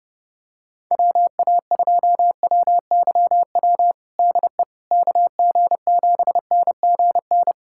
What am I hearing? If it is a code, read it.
Morse code, heard as a keyed tone, about 30 wpm, 700 Hz. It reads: WA2WYW DE KG7NGN
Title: USA Callsign DE Callsign - 6 Characters - 30wpm